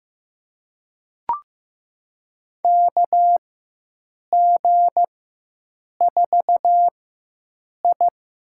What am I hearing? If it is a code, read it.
Morse code, heard as a keyed tone, about 15 wpm, spaced as for 7 wpm, 700 Hz. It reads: KG4I
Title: USA Call Signs - 6 Characters - 4x Character Spacing - 15wpm